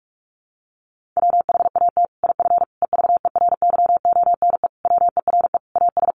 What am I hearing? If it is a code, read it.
Morse code, heard as a keyed tone, about 44 wpm, 700 Hz. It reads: WHAT IF EVERYONE WERE AS
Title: Sentences from Top 1400 Words 45wpm